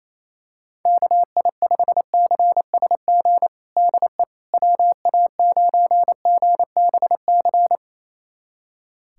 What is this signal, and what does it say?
Morse code, heard as a keyed tone, about 27 wpm, 700 Hz. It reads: KI5CSG DE WA9GBC
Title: USA Callsign DE Callsign - 6 Characters - 28wpm